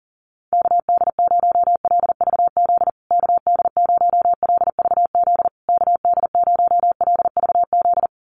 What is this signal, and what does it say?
Morse code, heard as a keyed tone, about 40 wpm, 700 Hz. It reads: KD0LVZ KD0LVZ KD0LVZ
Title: USA Call Signs Repeated 3x 40wpm